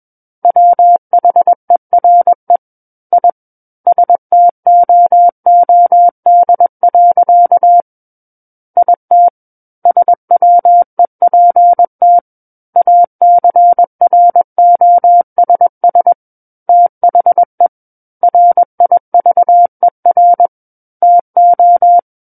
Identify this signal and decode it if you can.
Morse code, heard as a keyed tone, about 21 wpm, 700 Hz. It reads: WHERE I STOOD. IT SWEPT ACROSS THE RIVER TO
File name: Book One / 12-Chapter 12 - 21 WPM